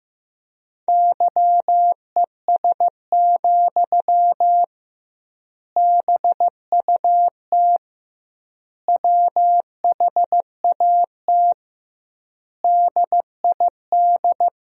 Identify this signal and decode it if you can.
Morse code, heard as a keyed tone, about 15 wpm, 700 Hz. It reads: YES, BUT WHAT DID
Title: 2x Word-Spacing - Sentences from Top 500 Words - 15wpm